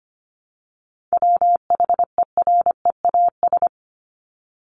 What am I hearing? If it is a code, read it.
Morse code, heard as a keyed tone, about 25 wpm, 700 Hz. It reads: WHEREAS
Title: Top 1501-1750 Words 25wpm